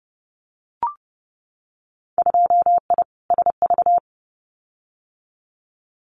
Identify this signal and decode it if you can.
Morse code, heard as a keyed tone, about 30 wpm, 700 Hz. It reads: 2I SV
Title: Winter Field Day Exchanges Encore 30wpm